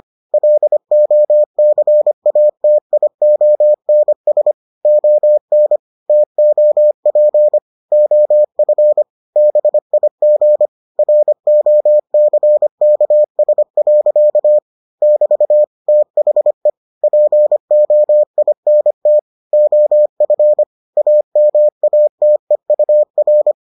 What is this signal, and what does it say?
Morse code, heard as a keyed tone, about 25 wpm, 600 Hz. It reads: LOCATIONS ON TOP OF BIG ROCKS. <BT> THE POINT OF AMATEUR